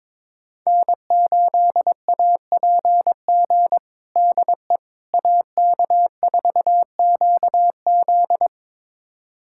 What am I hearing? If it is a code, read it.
Morse code, heard as a keyed tone, about 22 wpm, 700 Hz. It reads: N8APG DE AK4QZ
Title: USA Callsign DE Callsign - 5 Characters - 22wpm